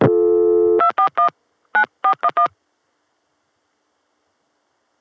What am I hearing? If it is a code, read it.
DTMF, heard as telephone touch tones, tones of about 100 ms. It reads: 241 9411